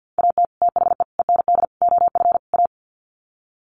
Morse code, heard as a keyed tone, about 50 wpm, 700 Hz. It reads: AT THE END OF A
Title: Sets of 5 Words 50wpm